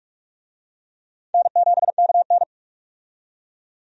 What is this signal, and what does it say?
Morse code, heard as a keyed tone, about 47 wpm, 700 Hz. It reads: N7KN
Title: USA Call Signs - 4 Characters - 45wpm